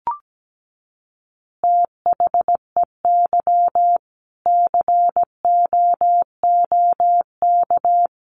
Morse code, heard as a keyed tone, about 17 wpm, 700 Hz. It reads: THEY COOK
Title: Easy Phrases 17wpm